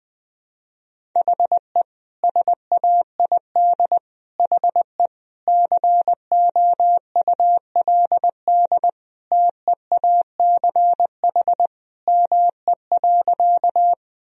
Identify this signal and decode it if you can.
Morse code, heard as a keyed tone, about 21 wpm, 700 Hz. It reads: HE SAID HE COULD TEACH ME.
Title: Sentences from Top 1000 Words - Review Part 2 - 20wpm